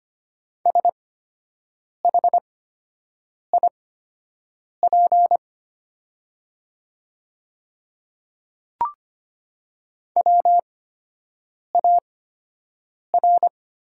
Morse code, heard as a keyed tone, about 25 wpm, spaced as for 7 wpm, 700 Hz. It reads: SHIP WAR
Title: 8x Character Spacing - Top 500 Words - 25wpm